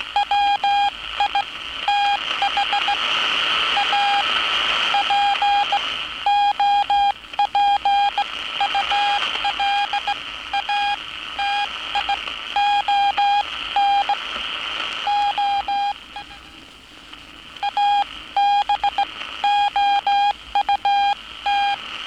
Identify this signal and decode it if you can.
Morse code, heard as a keyed tone, about 16 wpm, spaced as for 13 wpm, 810 Hz, with noise. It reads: WITH A POPULATION O ABOUT
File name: training / NAVPERS 11049to11060 / NAVPERS 11059 RA RB